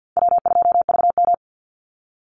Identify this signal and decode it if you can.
Morse code, heard as a keyed tone, about 50 wpm, 700 Hz. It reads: W2VM
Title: Call Signs Repeated 2x - 50wpm